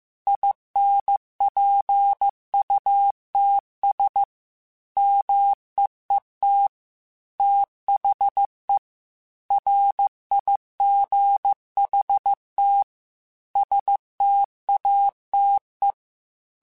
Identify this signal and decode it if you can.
Morse code, heard as a keyed tone, about 15 wpm, 800 Hz. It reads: INPUTS MEET THE RIGHT STATE
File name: Technical Subjects / Technical Subjects-mp3s / OR Gate @13